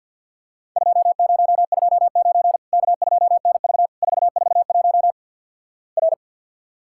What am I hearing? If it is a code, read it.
Morse code, heard as a keyed tone, about 45 wpm, 700 Hz. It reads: 2029 K2NV 441 R
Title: Worked All Europe DX Contest - QTC Exchange - 50wpm